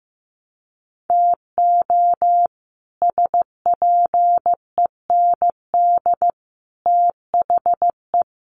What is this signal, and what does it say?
Morse code, heard as a keyed tone, about 15 wpm, 700 Hz. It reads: TO SPEND THE
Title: Sets of 5 Words Encore2 15wpm